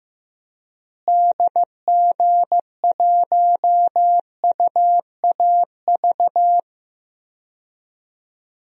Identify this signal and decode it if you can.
Morse code, heard as a keyed tone, about 15 wpm, 700 Hz. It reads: DG1UAV
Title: DXCC-100 Call Signs - 6 Characters - Repeated 2x - 15wpm